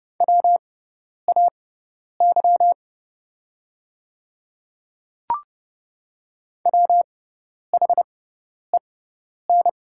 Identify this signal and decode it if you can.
Morse code, heard as a keyed tone, about 30 wpm, spaced as for 10 wpm, 700 Hz. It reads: WAY WHEN
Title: 6x Character Spacing - Top 100 Words - 30wpm